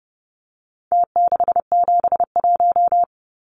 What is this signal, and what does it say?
Morse code, heard as a keyed tone, about 30 wpm, 700 Hz. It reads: T671
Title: Numbers - Contest Style - 4-Digit 30wpm